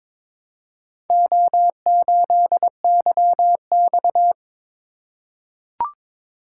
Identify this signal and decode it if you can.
Morse code, heard as a keyed tone, about 22 wpm, 700 Hz. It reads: O8YX
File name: practice / 22wpm / Four Letter-Number Combinations - 22wpm